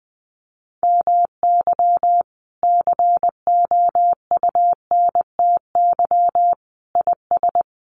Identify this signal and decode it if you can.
Morse code, heard as a keyed tone, about 20 wpm, 700 Hz. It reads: MY COUNTY IS
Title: Sentences from Top 1200 Words 20wpm